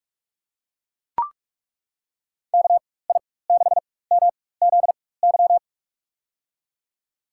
Morse code, heard as a keyed tone, about 41 wpm, spaced as for 20 wpm, 700 Hz. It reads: KI6MZY
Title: USA Call Signs - 6 Characters - 4x Character Spacing - 45wpm